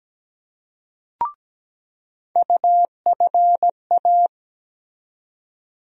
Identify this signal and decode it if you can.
Morse code, heard as a keyed tone, about 17 wpm, 700 Hz. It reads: UFA
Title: USA Call Sign Postfixes - 3 Letters 17wpm